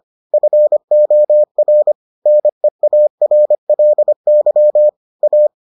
Morse code, heard as a keyed tone, about 25 wpm, 600 Hz. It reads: FOR NEARLY A